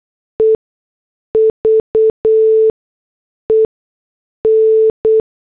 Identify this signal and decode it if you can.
Morse code, heard as a keyed tone, about 8 wpm, spaced as for 6 wpm, 440 Hz. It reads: EVEN
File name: morse / headlines